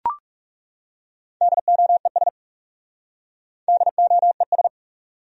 Morse code, heard as a keyed tone, about 40 wpm, 700 Hz. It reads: DOES DOES